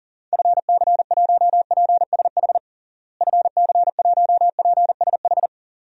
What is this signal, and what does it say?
Morse code, heard as a keyed tone, about 40 wpm, 700 Hz. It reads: FC1PSH FC1PSH
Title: DXCC-100 Call Signs - 6 Characters - Repeated 2x - 40wpm